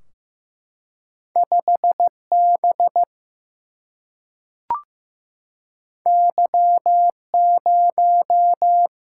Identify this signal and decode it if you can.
Morse code, heard as a keyed tone, about 15 wpm, 700 Hz. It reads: 5 B Y 0
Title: Single-Letter Single-Number 15wpm